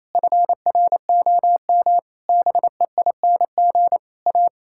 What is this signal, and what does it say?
Morse code, heard as a keyed tone, about 28 wpm, 700 Hz. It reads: FROM BEING A
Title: Sentences from Top 1000 Words - Review Part 6 - 28wpm